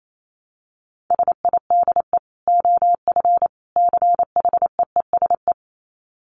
Morse code, heard as a keyed tone, about 28 wpm, 700 Hz. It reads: SIDE OF CHEESE
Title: Dit Challenge 28wpm